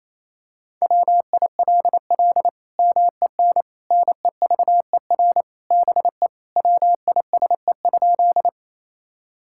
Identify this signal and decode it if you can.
Morse code, heard as a keyed tone, about 28 wpm, 700 Hz. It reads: WILL MEN NEVER BE WISE?
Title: Sentences from Top 1600 Words 28wpm